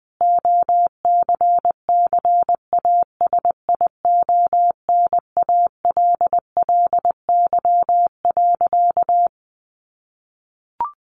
Morse code, heard as a keyed tone, about 19 wpm, 700 Hz. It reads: OCCASIONALLY.
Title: Sentences from Top 1300 Words 20wpm